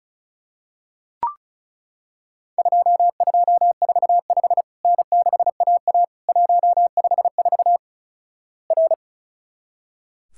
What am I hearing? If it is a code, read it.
Morse code, heard as a keyed tone, about 36 wpm, 700 Hz. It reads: 2245 N6AA 154 R
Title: Worked All Europe DX Contest - QTC Exchange - 35wpm